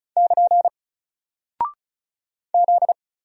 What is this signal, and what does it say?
Morse code, heard as a keyed tone, about 36 wpm, 700 Hz. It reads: <KN> Z
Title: Single Character (Letter:Number:Punctuation:Prosign) 35wpm